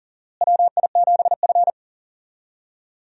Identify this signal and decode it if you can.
Morse code, heard as a keyed tone, about 40 wpm, 700 Hz. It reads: WI7F
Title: USA Call Signs - 4 Characters - Repeated 3x - 40wpm